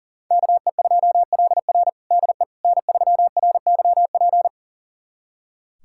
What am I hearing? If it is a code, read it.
Morse code, heard as a keyed tone, about 37 wpm, 700 Hz. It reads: KE2LR DE N3RYP
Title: USA Callsign DE Callsign - 5 Characters - 40wpm